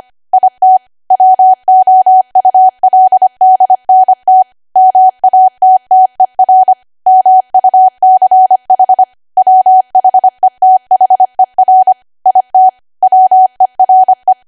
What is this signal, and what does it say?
Morse code, heard as a keyed tone, about 25 wpm, 750 Hz. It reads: IT WOULDNT MATTER MUCH WHETHER IT WERE
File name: Archive / 25 WPM / 25WPM